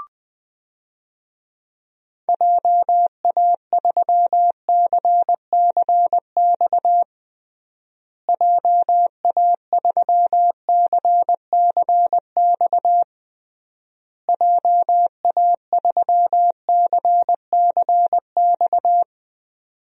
Morse code, heard as a keyed tone, about 20 wpm, 700 Hz. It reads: JA3CCX JA3CCX JA3CCX
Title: DXCC-100 Call Signs - 6 Characters - Repeated 3x - 20wpm